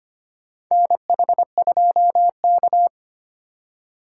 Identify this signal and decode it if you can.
Morse code, heard as a keyed tone, about 25 wpm, 700 Hz. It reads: NH2K
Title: USA Call Signs - 4 Characters - 3x Character Spacing - 25wpm